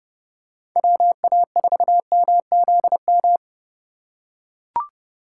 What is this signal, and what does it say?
Morse code, heard as a keyed tone, about 30 wpm, 700 Hz. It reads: WA4MZM
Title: USA Call Signs - 6 Characters - 4x Character Spacing - 30wpm